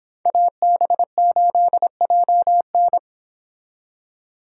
Morse code, heard as a keyed tone, about 28 wpm, 700 Hz. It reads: AB8JN